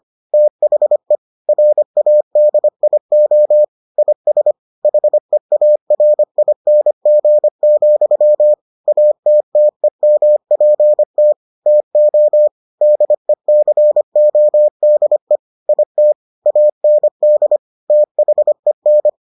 Morse code, heard as a keyed tone, about 25 wpm, 600 Hz. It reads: THE RADIO IS HEARING, ATTEMPT TO DECODE IT AND THEN